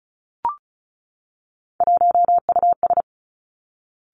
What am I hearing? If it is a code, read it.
Morse code, heard as a keyed tone, about 36 wpm, 700 Hz. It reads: 1US
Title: Three Challenging Letter-Number Combinations 35wpm